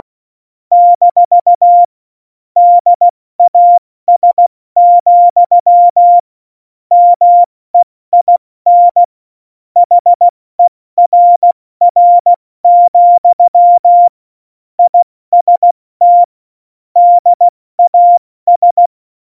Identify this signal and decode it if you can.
Morse code, heard as a keyed tone, about 16 wpm, 700 Hz. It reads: - DAS, MEIN HERR, IST DAS